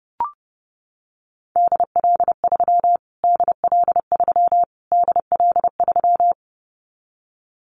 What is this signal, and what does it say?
Morse code, heard as a keyed tone, about 30 wpm, 700 Hz. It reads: DL3 DL3 DL3
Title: Callsign Prefixes for Top 25 DXCC - Repeated 3x - 30wpm